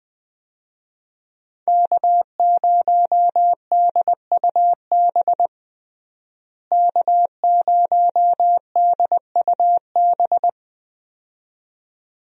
Morse code, heard as a keyed tone, about 20 wpm, 700 Hz. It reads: K0DUB K0DUB
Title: USA Call Signs - 5 Characters - Repeated 2x - 20wpm